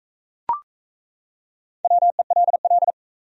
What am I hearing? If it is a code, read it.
Morse code, heard as a keyed tone, about 41 wpm, 700 Hz. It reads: WELL